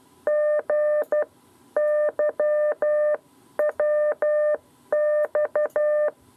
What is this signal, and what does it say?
Morse code, heard as a keyed tone, about 12 wpm, 600 Hz. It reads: G YWX
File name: 7b lezione cw 2